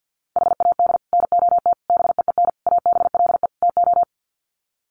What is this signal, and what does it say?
Morse code, heard as a keyed tone, about 50 wpm, 700 Hz. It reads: HAD NOT BEEN ABLE TO
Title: Sets of 5 Words Encore 50wpm